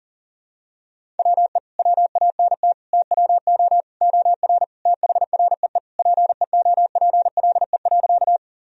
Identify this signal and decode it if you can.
Morse code, heard as a keyed tone, about 40 wpm, 700 Hz. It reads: WE WANT TWO OR THREE PEOPLE.
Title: Sentences from Top 1000 Words - Review Part 1 - 40wpm